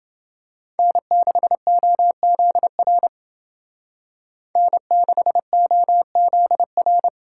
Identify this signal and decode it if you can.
Morse code, heard as a keyed tone, about 28 wpm, 700 Hz. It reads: N6OZR N6OZR